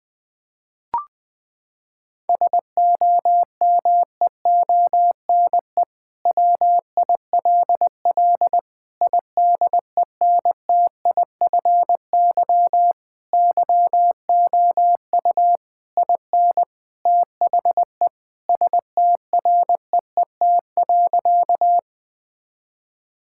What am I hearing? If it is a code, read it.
Morse code, heard as a keyed tone, about 20 wpm, 700 Hz. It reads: SOMEONE WILL IDENTIFY YOU IN THE STREET.
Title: Sentences from Top 1000 Words - Review Part 6 - 20wpm